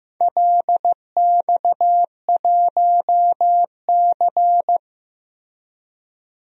Morse code, heard as a keyed tone, about 16 wpm, 700 Hz. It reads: LX1C
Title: DXCC-100 Call Signs - 4 Characters - Repeated 3x - 15wpm